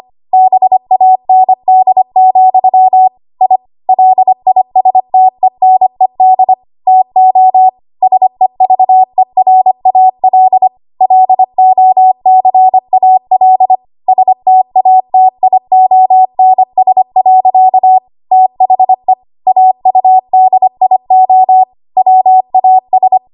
Morse code, heard as a keyed tone, about 25 wpm, 750 Hz. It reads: BAND, I LISTENED TO SEVERAL LOCAL STATIONS. THE AUDIO WAS
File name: Archive / 25 WPM / 25WPM